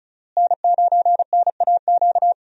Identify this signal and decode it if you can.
Morse code, heard as a keyed tone, about 36 wpm, 700 Hz. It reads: N9NAQ
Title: USA Call Signs - 5 Characters - Repeated 3x - 35wpm